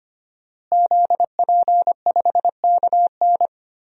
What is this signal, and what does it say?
Morse code, heard as a keyed tone, about 25 wpm, 700 Hz. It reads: ZP5KN
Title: DXCC-100 Call Signs - 5 Characters - Repeated 2x - 25wpm